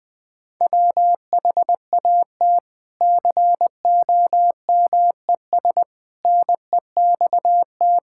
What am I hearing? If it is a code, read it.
Morse code, heard as a keyed tone, about 20 wpm, 700 Hz. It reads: WHAT COMES NEXT
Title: Sentences from Top 1000 Words - Review Part 5 - 20wpm